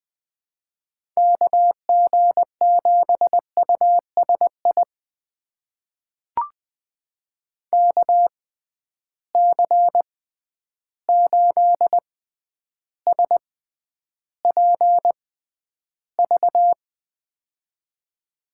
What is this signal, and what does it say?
Morse code, heard as a keyed tone, about 20 wpm, spaced as for 10 wpm, 700 Hz. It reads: KG7USI KC8SPV
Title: USA Call Signs - 6 Characters - 6x Character Spacing - 20wpm